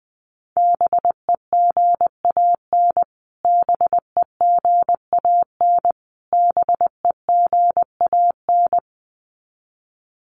Morse code, heard as a keyed tone, about 20 wpm, 700 Hz. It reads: BEGAN BEGAN BEGAN
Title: Top 300 Words Repeated 20wpm